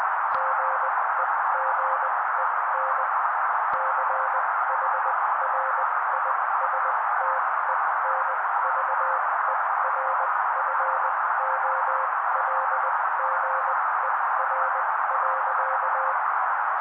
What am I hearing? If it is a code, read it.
Morse code, heard as a keyed tone, about 20 wpm, spaced as for 16 wpm, 530 Hz. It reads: GEGEN CHRISTENVERFOLGER.
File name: daily / 16wpm noise